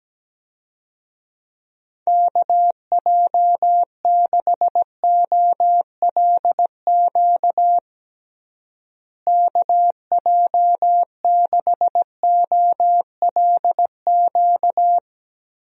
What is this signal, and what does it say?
Morse code, heard as a keyed tone, about 17 wpm, 700 Hz. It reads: KJ6OLQ KJ6OLQ
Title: USA Call Signs - 6 Characters - Repeated 2x - 17wpm